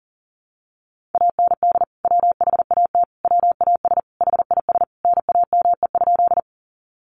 Morse code, heard as a keyed tone, about 40 wpm, 700 Hz. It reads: AND WHAT WAS HIS NAME?
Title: Sentences from Top 200 Words 40wpm